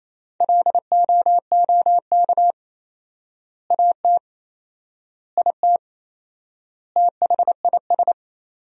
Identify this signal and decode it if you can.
Morse code, heard as a keyed tone, about 28 wpm, 700 Hz. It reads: LOOK AT IT THIS
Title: 4x Word-Spacing - Sentences from Top 100 Words - 28wpm